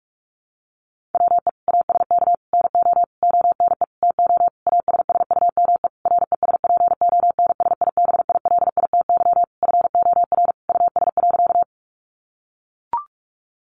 Morse code, heard as a keyed tone, about 44 wpm, 700 Hz. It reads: WE ASK NO ONE TO ASSUME RESPONSIBILITY FOR US.
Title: Sentences from Top 1500 Words 45wpm